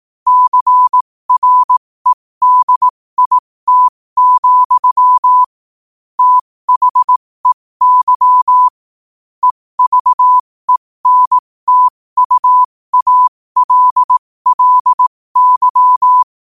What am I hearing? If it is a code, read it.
Morse code, heard as a keyed tone, about 18 wpm, spaced as for 15 wpm, 1000 Hz. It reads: CREDIT, THEY EVENTUALLY
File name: qotdfiles / 15wpm